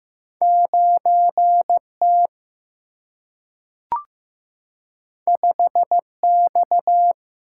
Morse code, heard as a keyed tone, about 15 wpm, 700 Hz. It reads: 9T 5X
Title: Single-Letter Single-Number 15wpm